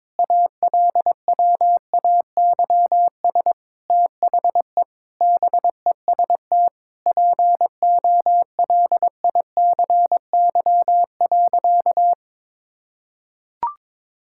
Morse code, heard as a keyed tone, about 22 wpm, 700 Hz. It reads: ALWAYS THE BEST POLICY.
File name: Difficult Sentences with Hints - 22wpm